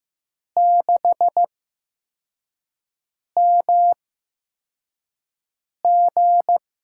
Morse code, heard as a keyed tone, about 15 wpm, spaced as for 4 wpm, 700 Hz. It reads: 6MG
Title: USA Call Signs - 5 Characters - 8x Character Spacing - 15wpm